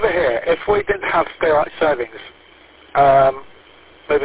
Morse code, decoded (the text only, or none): none